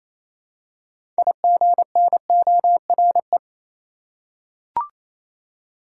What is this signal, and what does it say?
Morse code, heard as a keyed tone, about 28 wpm, 700 Hz. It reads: IGNORE
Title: Six Letter Words - 28wpm